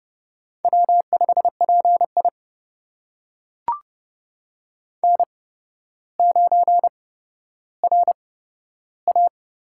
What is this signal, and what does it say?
Morse code, heard as a keyed tone, about 30 wpm, spaced as for 13 wpm, 700 Hz. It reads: W5PI N9RA
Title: USA Call Signs - 4 Characters - 8x Character Spacing - 30wpm